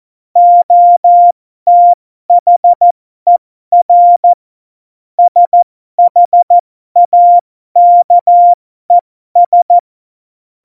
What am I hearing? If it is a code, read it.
Morse code, heard as a keyed tone, about 14 wpm, spaced as for 12 wpm, 700 Hz. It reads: OTHER SHAKES
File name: Morsekurs / Witze